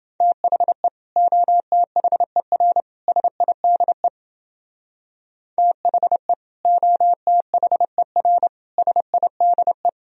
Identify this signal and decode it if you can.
Morse code, heard as a keyed tone, about 29 wpm, 700 Hz. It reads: THE OTHER SIDE THE OTHER SIDE